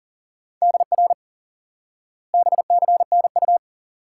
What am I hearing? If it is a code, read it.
Morse code, heard as a keyed tone, about 40 wpm, 700 Hz. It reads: DR BCNU
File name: IQE Course 3 - Lesson 89 - YAGI - Two Elements - 40wpm